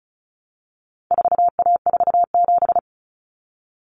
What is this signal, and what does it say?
Morse code, heard as a keyed tone, about 34 wpm, 700 Hz. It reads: 4A47
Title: Common Cut Numbers - 4-Digit 35wpm